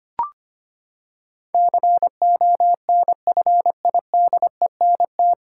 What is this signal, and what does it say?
Morse code, heard as a keyed tone, about 25 wpm, 700 Hz. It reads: CONFIDENT
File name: Nine Letter Words - 25wpm